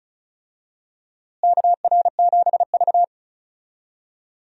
Morse code, heard as a keyed tone, about 36 wpm, 700 Hz. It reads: KR7V
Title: USA Call Signs - 4 Characters - Repeated 2x - 35wpm